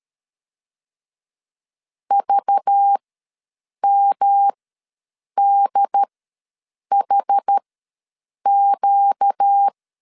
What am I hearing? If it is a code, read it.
Morse code, heard as a keyed tone, about 13 wpm, spaced as for 7 wpm, 800 Hz. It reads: VMDHQ
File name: LEZIONI / 5 LBVC / 5 LBV ws 12wpm 3015